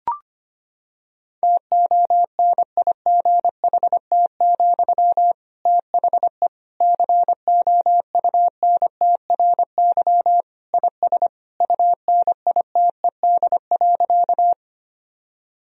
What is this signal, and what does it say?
Morse code, heard as a keyed tone, about 25 wpm, 700 Hz. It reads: TONIGHT, THE COUNTRY IS UNITED.
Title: Sentences from Top 1000 Words - Review Part 4 - 25wpm